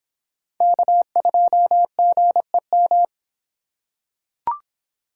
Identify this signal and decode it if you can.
Morse code, heard as a keyed tone, about 26 wpm, 700 Hz. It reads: K2GEM